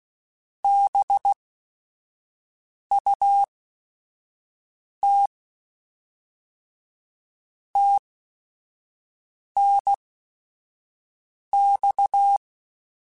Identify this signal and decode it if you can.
Morse code, heard as a keyed tone, about 16 wpm, spaced as for 5 wpm, 800 Hz. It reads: BUT TNX